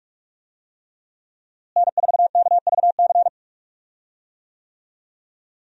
Morse code, heard as a keyed tone, about 46 wpm, 700 Hz. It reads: N4KVC
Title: USA Call Signs - 5 Characters - Repeated 3x - 45wpm